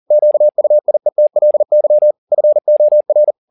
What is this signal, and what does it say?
Morse code, heard as a keyed tone, about 40 wpm, 600 Hz. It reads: QUIETLY FOR
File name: the old man and the sea 2 40wpm